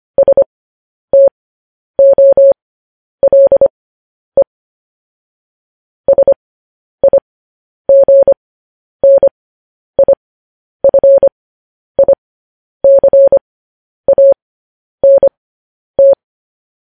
Morse code, heard as a keyed tone, about 25 wpm, spaced as for 10 wpm, 550 Hz. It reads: STOLE SIGNIFICANT
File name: podcast / News headlines-1x-25x10